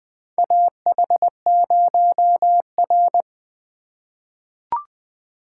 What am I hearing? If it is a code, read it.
Morse code, heard as a keyed tone, about 21 wpm, 700 Hz. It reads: AH0R
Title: USA Call Signs - 4 Characters - 4x Character Spacing - 20wpm